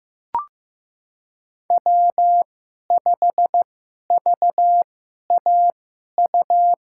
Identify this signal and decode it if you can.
Morse code, heard as a keyed tone, about 15 wpm, spaced as for 11 wpm, 700 Hz. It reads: W5VAU
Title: USA Call Signs - 5 Characters - 2x Character Spacing - 15wpm